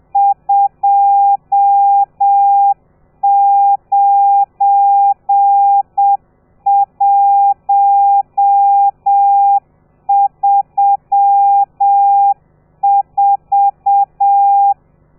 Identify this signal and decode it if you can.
Morse code, heard as a keyed tone, about 7 wpm, 800 Hz. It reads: 29134
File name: telegrafia / i2tfj / Lez12 30